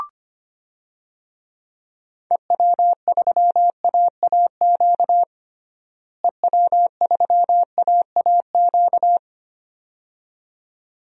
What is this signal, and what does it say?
Morse code, heard as a keyed tone, about 25 wpm, 700 Hz. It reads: EW3AAQ EW3AAQ
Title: DXCC-100 Call Signs - 6 Characters - Repeated 2x - 25wpm